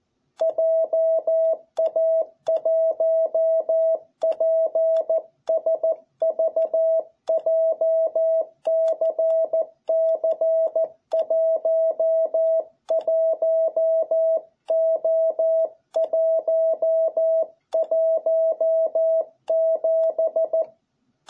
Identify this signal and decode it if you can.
Morse code, heard as a keyed tone, about 14 wpm, 660 Hz. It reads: JA1PSV JCC11O117